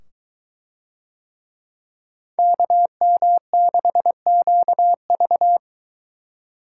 Morse code, heard as a keyed tone, about 23 wpm, 700 Hz. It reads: KM6QV